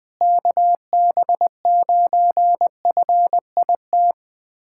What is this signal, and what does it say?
Morse code, heard as a keyed tone, about 20 wpm, 700 Hz. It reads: KB9FIT
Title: USA Callsign DE Callsign 20wpm